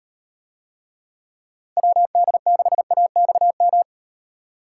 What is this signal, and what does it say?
Morse code, heard as a keyed tone, about 38 wpm, 700 Hz. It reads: WD6AXM